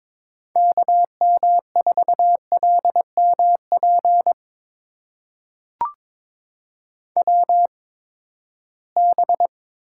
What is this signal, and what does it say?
Morse code, heard as a keyed tone, about 21 wpm, spaced as for 9 wpm, 700 Hz. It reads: KM4LM PWB
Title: USA Call Signs - 8x Character Spacing - 22wpm